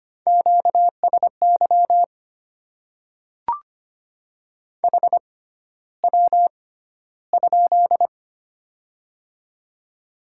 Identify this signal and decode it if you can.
Morse code, heard as a keyed tone, about 25 wpm, spaced as for 13 wpm, 700 Hz. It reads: QSY HW?
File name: Top 90 QSO Elements - 6x Character Spacing - 25wpm